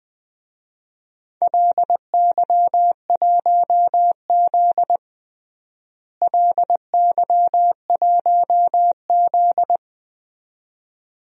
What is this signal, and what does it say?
Morse code, heard as a keyed tone, about 20 wpm, 700 Hz. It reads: LY1Z LY1Z
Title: DXCC-100 Call Signs - 4 Characters - Repeated 2x - 20wpm